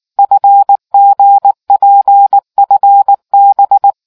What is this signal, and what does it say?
Morse code, heard as a keyed tone, about 19 wpm, 800 Hz. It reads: FGPFB